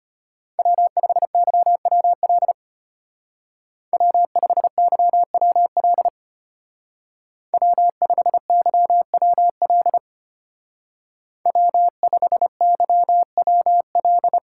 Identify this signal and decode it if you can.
Morse code, heard as a keyed tone, about 29 wpm, 700 Hz. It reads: W5YWL W5YWL W5YWL W5YWL
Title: Speed Racing - USA 1x3 Call Signs 25wpm